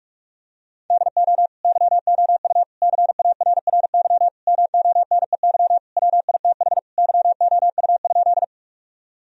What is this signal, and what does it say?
Morse code, heard as a keyed tone, about 41 wpm, 700 Hz. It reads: DO YOU CARRY MONEY WITH YOU?
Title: Sentences from Top 1000 Words - Review Part 1 - 45wpm